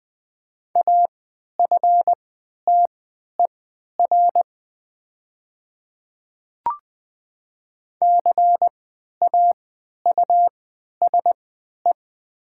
Morse code, heard as a keyed tone, about 20 wpm, spaced as for 11 wpm, 700 Hz. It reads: AFTER CAUSE
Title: 3x Character Spacing - Top 200 Words - 20wpm